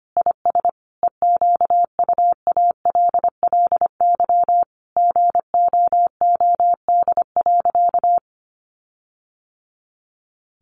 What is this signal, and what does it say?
Morse code, heard as a keyed tone, about 25 wpm, 700 Hz. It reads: IS EQUALLY GOOD.
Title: Sentences from Top 1400 Words 25wpm